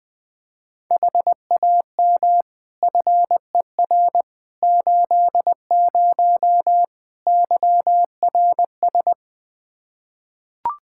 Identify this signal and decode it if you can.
Morse code, heard as a keyed tone, about 20 wpm, 700 Hz. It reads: HAM FER 80 YRS
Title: Practice QSO - Exchange 3 - More Info - 20wpm